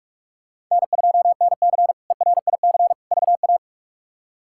Morse code, heard as a keyed tone, about 46 wpm, 700 Hz. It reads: N2NC ERIC VA
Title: K1USN SST QSO Practice - 45wpm